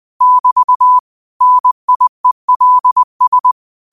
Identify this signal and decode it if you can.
Morse code, heard as a keyed tone, about 20 wpm, 1000 Hz. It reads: <BT> NIELS
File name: qotdfiles / 20wpm